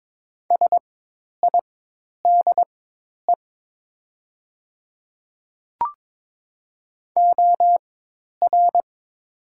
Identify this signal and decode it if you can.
Morse code, heard as a keyed tone, about 22 wpm, spaced as for 10 wpm, 700 Hz. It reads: SIDE OR
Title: 4x Character Spacing - Top 500 Words - 22wpm